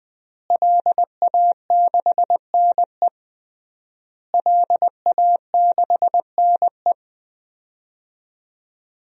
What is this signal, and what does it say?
Morse code, heard as a keyed tone, about 20 wpm, 700 Hz. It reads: LA6NE LA6NE
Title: DXCC-100 Call Signs - 5 Characters - Repeated 3x - 20wpm